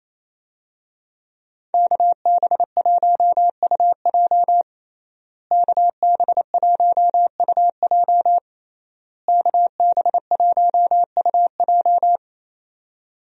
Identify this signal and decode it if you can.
Morse code, heard as a keyed tone, about 28 wpm, 700 Hz. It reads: KB1UJ KB1UJ KB1UJ
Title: USA Call Signs - 5 Characters - Repeated 3x - 28wpm